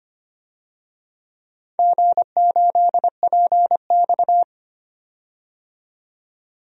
Morse code, heard as a keyed tone, about 25 wpm, 700 Hz. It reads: G8PX
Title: DXCC-100 Call Signs - 4 Characters - 25wpm